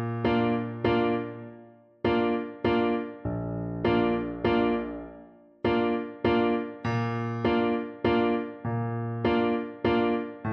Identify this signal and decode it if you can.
There is no code.